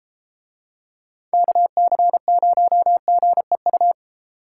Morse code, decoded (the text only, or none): KC0GEU